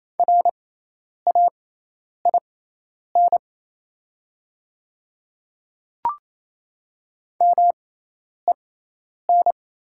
Morse code, heard as a keyed tone, about 28 wpm, spaced as for 10 wpm, 700 Hz. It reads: RAIN MEN
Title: 6x Character Spacing - Top 300 Words - 28wpm